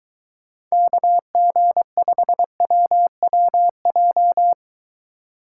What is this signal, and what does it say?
Morse code, heard as a keyed tone, about 23 wpm, 700 Hz. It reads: KG5WWJ